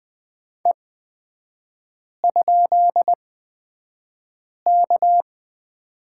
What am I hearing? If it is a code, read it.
Morse code, heard as a keyed tone, about 20 wpm, spaced as for 5 wpm, 700 Hz. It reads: E?K
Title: 224 Lesson 36e - BT - Three Characters 20wpm